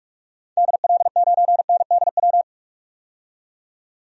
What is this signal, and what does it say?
Morse code, heard as a keyed tone, about 41 wpm, 700 Hz. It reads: DL9NDW
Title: DXCC-100 Call Signs - 6 Characters - 45wpm